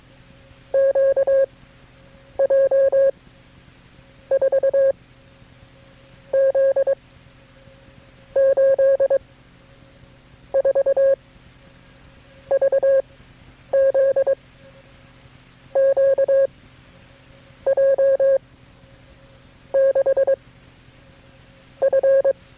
Morse code, heard as a keyed tone, about 22 wpm, spaced as for 9 wpm, 540 Hz. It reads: QJ 4 Z 8 4 VZ Q J 6 F
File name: KB6MT-tape3-side1